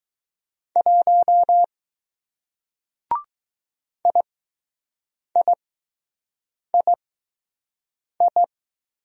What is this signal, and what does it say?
Morse code, heard as a keyed tone, about 20 wpm, spaced as for 6 wpm, 700 Hz. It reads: 1 IIII